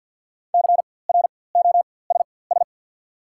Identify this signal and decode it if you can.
Morse code, heard as a keyed tone, about 54 wpm, spaced as for 24 wpm, 700 Hz. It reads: CROSS
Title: 4x Character Spacing - Top 500 Words - 50wpm